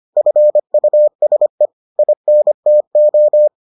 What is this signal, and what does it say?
Morse code, heard as a keyed tone, about 25 wpm, 600 Hz. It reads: FUSE INTO